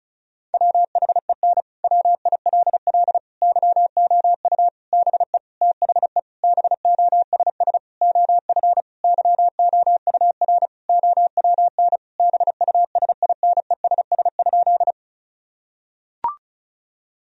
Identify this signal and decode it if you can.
Morse code, heard as a keyed tone, about 36 wpm, 700 Hz. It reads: WHEN WILL YOU BE THE BOSS OF YOUR OWN BUSINESS?
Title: Sentences from Top 1000 Words - Review Part 4 - 35wpm